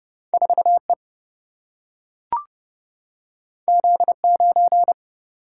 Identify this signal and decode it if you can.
Morse code, heard as a keyed tone, about 30 wpm, 700 Hz. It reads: 4E Z9
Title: Single-Letter Single-Number 30wpm